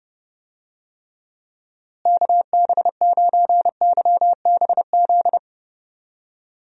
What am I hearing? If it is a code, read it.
Morse code, heard as a keyed tone, about 30 wpm, 700 Hz. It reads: KB9YBZ
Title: USA Call Signs - 6 Characters - Repeated 3x - 30wpm